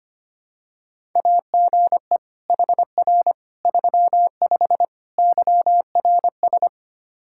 Morse code, heard as a keyed tone, about 25 wpm, 700 Hz. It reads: AGE HR 35 YRS
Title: Practice QSO - Exchange 3 - More Info - 25wpm